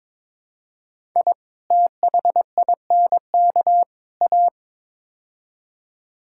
Morse code, heard as a keyed tone, about 22 wpm, 700 Hz. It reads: I THINK A
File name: practice / 22wpm / Build A Phrase - 5-Word Phrases - 22wpm